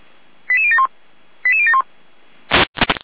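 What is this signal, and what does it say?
Five-tone selective call (ZVEI), heard as quick runs of five tones, tones of about 70 ms. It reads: 88071 88071
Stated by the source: Five-tone ZVEI selective call 85382, received by SDR.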